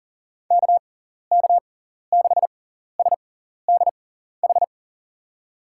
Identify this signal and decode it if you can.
Morse code, heard as a keyed tone, about 40 wpm, spaced as for 14 wpm, 700 Hz. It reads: KK6SDH
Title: USA Call Signs - 6x Character Spacing - 40wpm